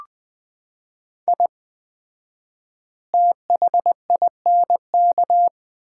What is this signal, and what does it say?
Morse code, heard as a keyed tone, about 20 wpm, 700 Hz. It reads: I THINK
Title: 4x Word-Spacing - Sets of 5 Words - 20wpm